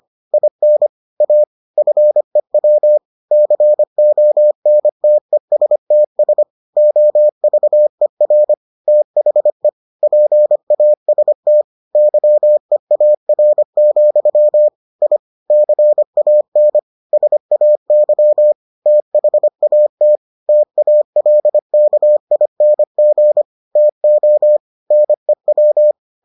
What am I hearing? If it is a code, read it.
Morse code, heard as a keyed tone, about 25 wpm, 600 Hz. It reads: IN A FEW CONTESTS OVER THE PAST YEAR, I CAN SAY THAT TALKING TO NEW